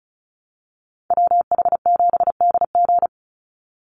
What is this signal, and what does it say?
Morse code, heard as a keyed tone, about 34 wpm, 700 Hz. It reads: WH7DG
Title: USA Call Signs 35wpm